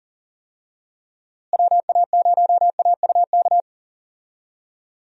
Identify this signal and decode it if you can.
Morse code, heard as a keyed tone, about 40 wpm, 700 Hz. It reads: WA0AUK
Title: USA Call Signs - 6 Characters - Repeated 2x - 40wpm